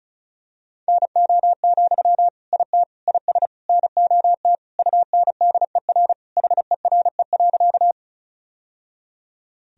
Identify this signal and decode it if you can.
Morse code, heard as a keyed tone, about 36 wpm, 700 Hz. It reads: NO, IT IS NOT UNDER HERE.
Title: Sentences from Top 1000 Words - Review Part 1 - 35wpm